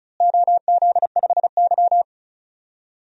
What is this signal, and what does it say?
Morse code, heard as a keyed tone, about 34 wpm, 700 Hz. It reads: OZ5Y
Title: DXCC-100 Call Signs - 4 Characters - Repeated 2x - 35wpm